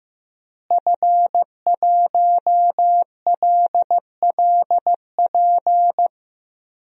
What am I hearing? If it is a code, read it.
Morse code, heard as a keyed tone, about 15 wpm, 700 Hz. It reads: F1LLP
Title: DXCC-100 Call Signs - 5 Characters - Repeated 2x - 15wpm